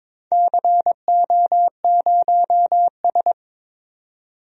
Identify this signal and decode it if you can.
Morse code, heard as a keyed tone, about 22 wpm, 700 Hz. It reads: CO0S
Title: DXCC-100 Call Signs - 4 Characters - 22wpm